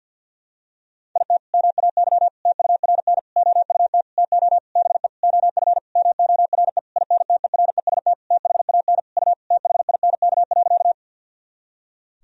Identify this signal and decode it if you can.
Morse code, heard as a keyed tone, about 50 wpm, 700 Hz. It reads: IT MAY TURN OUT TO BE OF MORE INTEREST THAN U THINK.
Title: Sentences from Top 400 Words and Abbreviations 50wpm